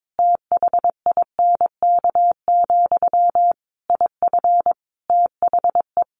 Morse code, heard as a keyed tone, about 22 wpm, 700 Hz. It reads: THINK, IF THE
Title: A Sound Of Thunder 22wpm